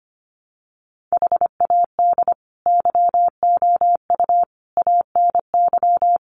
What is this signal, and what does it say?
Morse code, heard as a keyed tone, about 25 wpm, 700 Hz. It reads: HAD YOU ANY
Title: Sentences from Top 1500 Words 25wpm